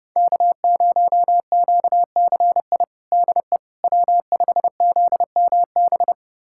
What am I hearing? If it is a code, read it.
Morse code, heard as a keyed tone, about 30 wpm, 700 Hz. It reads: K0QCI DE W5ZMB
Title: USA Callsign DE Callsign - 5 Characters - 30wpm